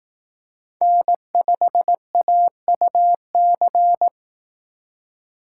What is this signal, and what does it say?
Morse code, heard as a keyed tone, about 18 wpm, 700 Hz. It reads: N5AUC
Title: Speed Racing - USA Call Signs 15wpm